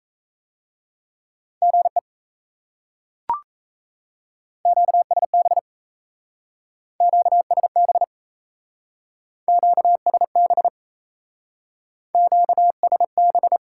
Morse code, heard as a keyed tone, about 37 wpm, 700 Hz. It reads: GE QSB QSB QSB QSB